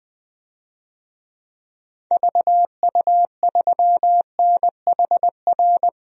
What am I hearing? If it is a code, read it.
Morse code, heard as a keyed tone, about 20 wpm, 700 Hz. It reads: VU3NHR
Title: DXCC-100 Call Signs - 6 Characters - 20wpm